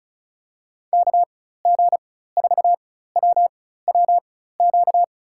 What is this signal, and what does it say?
Morse code, heard as a keyed tone, about 36 wpm, spaced as for 16 wpm, 700 Hz. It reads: KG4WWQ
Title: USA Call Signs - 6 Characters - 4x Character Spacing - 35wpm